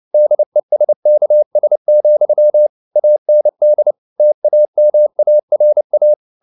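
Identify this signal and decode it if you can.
Morse code, heard as a keyed tone, about 29 wpm, 600 Hz. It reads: DESKS, AND TAMARA
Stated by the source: Morse code at 29 wpm, sending Ken Follett's novel Never in English, from CW med Gnister.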